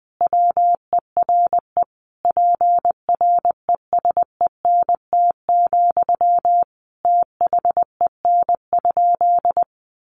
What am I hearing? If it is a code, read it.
Morse code, heard as a keyed tone, about 20 wpm, 700 Hz. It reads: WERE PRESENT, THEN?
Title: Sentences from Top 800 Words 20wpm